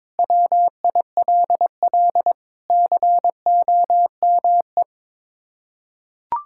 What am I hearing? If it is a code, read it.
Morse code, heard as a keyed tone, about 22 wpm, 700 Hz. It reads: WILL COME
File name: practice / 22wpm / Sets of 7 Words - 22wpm